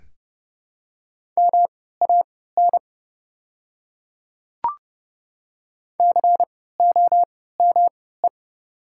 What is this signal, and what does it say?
Morse code, heard as a keyed tone, about 32 wpm, spaced as for 17 wpm, 700 Hz. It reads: MAN COME
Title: 3x Character Spacing - Top 300 Words - 30wpm